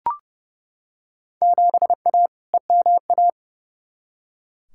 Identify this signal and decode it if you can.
Morse code, heard as a keyed tone, about 32 wpm, 700 Hz. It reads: 7A EMA
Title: Field Day Exchanges Encore - 30wpm